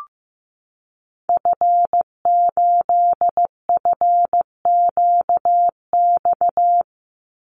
Morse code, heard as a keyed tone, about 15 wpm, 700 Hz. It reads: F8FQX
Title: Call Signs Repeated 3x - 15wpm